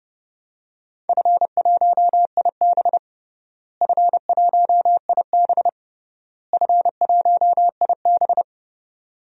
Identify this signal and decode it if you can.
Morse code, heard as a keyed tone, about 31 wpm, 700 Hz. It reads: F1IB F1IB F1IB
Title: DXCC-100 Call Signs - 4 Characters - Repeated 3x - 30wpm